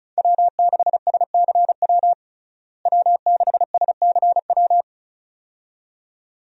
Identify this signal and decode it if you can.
Morse code, heard as a keyed tone, about 34 wpm, 700 Hz. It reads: W6SCW W6SCW
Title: USA Call Signs - 5 Characters - Repeated 2x - 35wpm